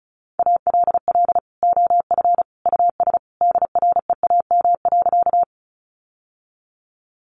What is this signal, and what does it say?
Morse code, heard as a keyed tone, about 34 wpm, 700 Hz. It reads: ALL OF US DREAM.
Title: Sentences from Top 1500 Words 35wpm